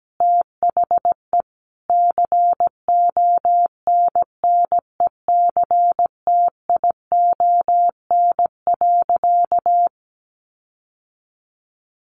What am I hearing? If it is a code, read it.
Morse code, heard as a keyed tone, about 17 wpm, 700 Hz. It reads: THE CONNECTION.
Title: Sentences from Top 1300 Words 17wpm